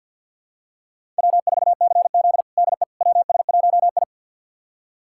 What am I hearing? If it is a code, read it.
Morse code, heard as a keyed tone, about 50 wpm, 700 Hz. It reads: W4CZ DE WS1I
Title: USA Callsign DE Callsign - 4 Characters - 50wpm